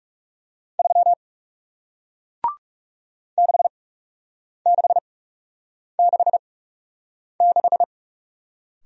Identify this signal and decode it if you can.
Morse code, heard as a keyed tone, about 38 wpm, spaced as for 8 wpm, 700 Hz. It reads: 3 6666